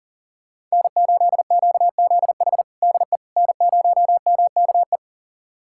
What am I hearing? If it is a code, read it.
Morse code, heard as a keyed tone, about 40 wpm, 700 Hz. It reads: N8QZH DE N0MKE
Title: USA Callsign DE Callsign - 5 Characters - 40wpm